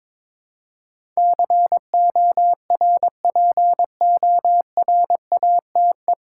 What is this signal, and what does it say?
Morse code, heard as a keyed tone, about 22 wpm, 700 Hz. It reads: CORPORATE
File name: 9 Letter Words - First 3 Letters Given - 22wpm